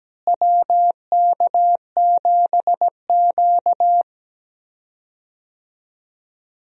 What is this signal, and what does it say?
Morse code, heard as a keyed tone, about 17 wpm, 700 Hz. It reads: WK7Q
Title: USA Call Signs - 4 Characters - Repeated 3x - 17wpm